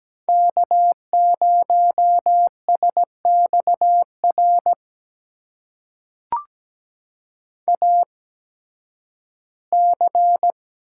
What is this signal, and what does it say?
Morse code, heard as a keyed tone, about 17 wpm, spaced as for 11 wpm, 700 Hz. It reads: K0SXR AC
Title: USA Call Signs - 5 Characters - 8x Character Spacing - 17wpm